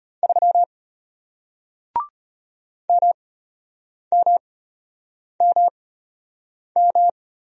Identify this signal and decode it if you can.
Morse code, heard as a keyed tone, about 34 wpm, spaced as for 8 wpm, 700 Hz. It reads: 3 MMMM